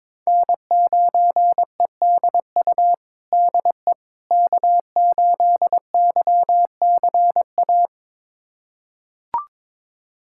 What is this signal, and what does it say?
Morse code, heard as a keyed tone, about 22 wpm, 700 Hz. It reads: N9EDU DE K8YCA
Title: USA Callsign DE Callsign - 5 Characters - 22wpm